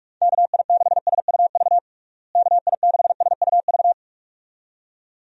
Morse code, heard as a keyed tone, about 46 wpm, 700 Hz. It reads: KI6SUV KI6SUV
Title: USA Call Signs - 6 Characters - Repeated 2x - 45wpm